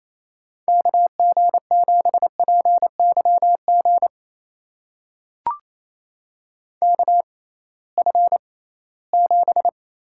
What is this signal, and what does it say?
Morse code, heard as a keyed tone, about 28 wpm, spaced as for 18 wpm, 700 Hz. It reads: KG7PYG KF7
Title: USA Call Signs - 6 Characters - 6x Character Spacing - 28wpm